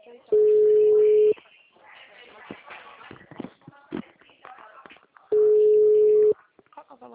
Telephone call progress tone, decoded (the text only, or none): ringback tone